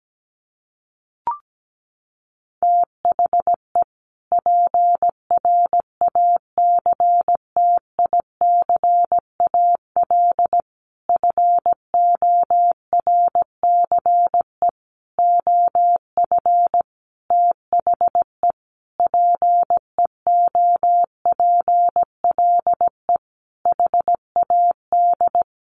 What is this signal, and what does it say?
Morse code, heard as a keyed tone, about 17 wpm, 700 Hz. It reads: THE PRACTICAL FORCE OF THE PEOPLE HAD
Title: Sentences from Top 1400 Words 17wpm